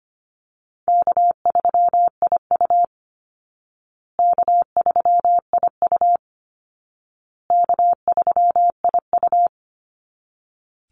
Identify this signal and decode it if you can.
Morse code, heard as a keyed tone, about 25 wpm, 700 Hz. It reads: K3IU K3IU K3IU
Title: Easy Call Signs Repeated 3x - 25wpm